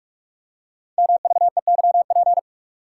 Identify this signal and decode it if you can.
Morse code, heard as a keyed tone, about 44 wpm, 700 Hz. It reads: MVEYP
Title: Code Groups - 5 Letters - 45wpm